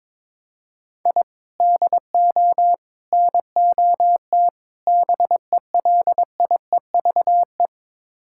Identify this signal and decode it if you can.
Morse code, heard as a keyed tone, about 22 wpm, 700 Hz. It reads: I DO NOT BELIEVE
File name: practice / 22wpm / Build A Phrase - 5-Word Phrases - 22wpm